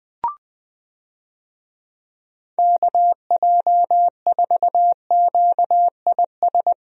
Morse code, heard as a keyed tone, about 21 wpm, 700 Hz. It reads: KJ4QIS
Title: USA Call Signs - 6 Characters - 20wpm